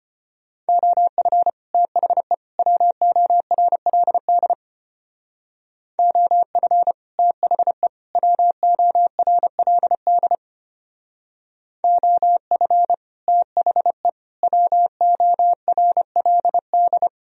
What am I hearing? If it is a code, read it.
Morse code, heard as a keyed tone, about 32 wpm, 700 Hz. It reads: OF THE WORLD OF THE WORLD OF THE WORLD